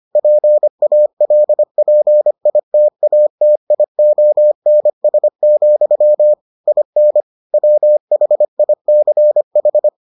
Morse code, heard as a keyed tone, about 25 wpm, 600 Hz. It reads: PALPITATIONS, IN WHICH